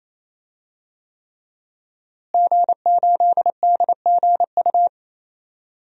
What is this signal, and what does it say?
Morse code, heard as a keyed tone, about 28 wpm, 700 Hz. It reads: G8DGU
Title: DXCC-100 Call Signs - 5 Characters - Repeated 3x - 28wpm